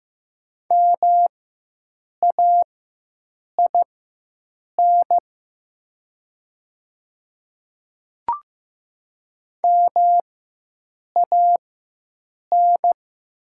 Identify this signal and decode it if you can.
Morse code, heard as a keyed tone, about 15 wpm, spaced as for 7 wpm, 700 Hz. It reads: MAIN MAN
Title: 4x Character Spacing - Top 300 Words - 15wpm